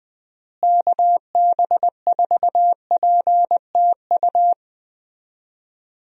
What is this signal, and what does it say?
Morse code, heard as a keyed tone, about 20 wpm, 700 Hz. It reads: KB4PTU
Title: USA Call Signs - 6 Characters - Repeated 2x - 20wpm